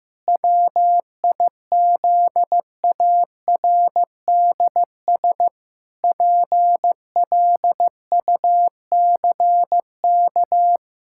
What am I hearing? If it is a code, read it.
Morse code, heard as a keyed tone, about 15 wpm, 700 Hz. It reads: WIZARDS PLUCK
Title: Pangram Challenge v2 - 15wpm